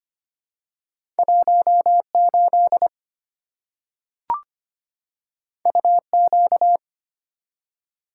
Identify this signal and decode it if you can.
Morse code, heard as a keyed tone, about 25 wpm, 700 Hz. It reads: 18 UQ
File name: Two Letter-Number Combinations - 25wpm